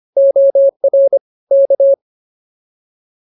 Morse code, heard as a keyed tone, about 24 wpm, 550 Hz. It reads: OR K